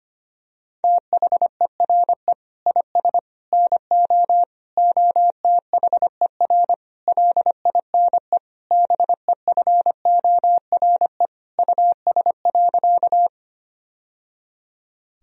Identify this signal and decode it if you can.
Morse code, heard as a keyed tone, about 25 wpm, 700 Hz. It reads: THERE IS NO OTHER LINE BEFORE US.
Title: Sentences from Top 1000 Words - Review Part 1 - 25wpm